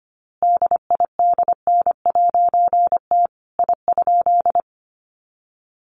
Morse code, heard as a keyed tone, about 25 wpm, 700 Hz. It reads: DIDN'T I?
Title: A Sound Of Thunder 25wpm